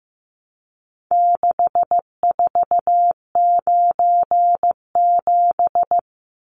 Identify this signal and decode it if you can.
Morse code, heard as a keyed tone, about 14 wpm, 700 Hz. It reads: 6497
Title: Numbers - 4-Digit 15wpm